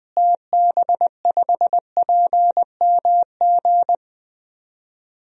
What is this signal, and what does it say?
Morse code, heard as a keyed tone, about 20 wpm, 700 Hz. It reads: TB5PMG
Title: DXCC-100 Call Signs - 6 Characters - Repeated 3x - 20wpm